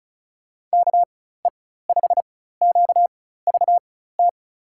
Morse code, heard as a keyed tone, about 34 wpm, spaced as for 16 wpm, 700 Hz. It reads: KE5QVT
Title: USA Call Signs - 6 Characters - 4x Character Spacing - 35wpm